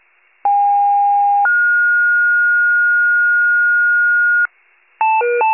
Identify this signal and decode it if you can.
Two-tone page, heard as a one-second tone, then a three-second tone, about 810 then 1430 Hz, with noise. It is heard from 0.4 to 4.5 s.
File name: pager / 73 FIRE / wcalert